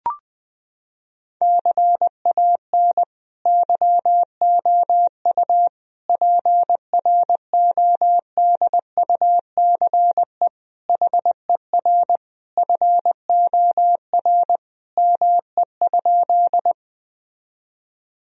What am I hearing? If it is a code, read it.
Morse code, heard as a keyed tone, about 21 wpm, 700 Hz. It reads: CAN YOU PRODUCE HER FOR ME?
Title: Sentences from Top 1000 Words - Review Part 3 - 20wpm